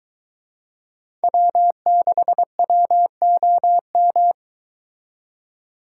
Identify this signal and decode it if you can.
Morse code, heard as a keyed tone, about 22 wpm, 700 Hz. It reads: W6WOM